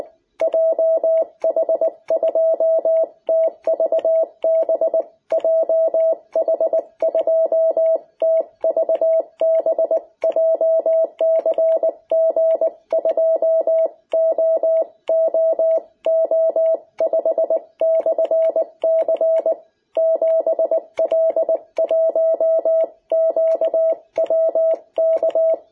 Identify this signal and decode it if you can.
Morse code, heard as a keyed tone, about 20 wpm, 660 Hz. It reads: JH2TVB JH2TVB JCG2OOO5/C 7L1QWK
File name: jh8cbh / 2021cw